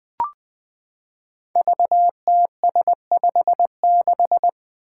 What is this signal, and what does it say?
Morse code, heard as a keyed tone, about 20 wpm, 700 Hz. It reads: VTS56
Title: Five Challenging Letter-Number Combinations 20wpm